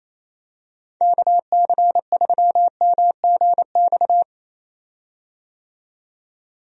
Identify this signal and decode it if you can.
Morse code, heard as a keyed tone, about 28 wpm, 700 Hz. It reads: KC3MGX
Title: USA Call Signs - 6 Characters - 28wpm